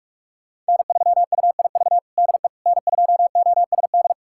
Morse code, heard as a keyed tone, about 46 wpm, 700 Hz. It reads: N3UIV DE N2OSD
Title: USA Callsign DE Callsign - 5 Characters - 45wpm